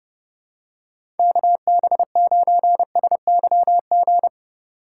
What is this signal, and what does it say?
Morse code, heard as a keyed tone, about 30 wpm, 700 Hz. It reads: KB9SYG
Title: USA Call Signs - 2x Character Spacing - 30wpm